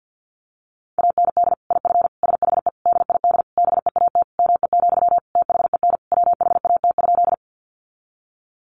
Morse code, heard as a keyed tone, about 50 wpm, 700 Hz. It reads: AND IF SHE DID BEAT ME, THEN WHAT?
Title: Sentences from Top 1300 Words 50wpm